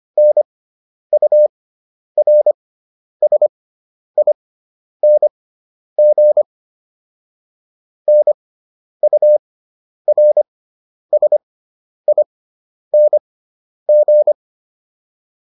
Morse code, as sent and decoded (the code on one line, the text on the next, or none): -. ..- .-. ... .. -. --. / -. ..- .-. ... .. -. --.
NURSING NURSING